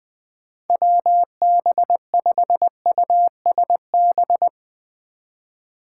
Morse code, heard as a keyed tone, about 21 wpm, 700 Hz. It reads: WB5USB
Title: 2023 SOTA Activator Call Signs - 20wpm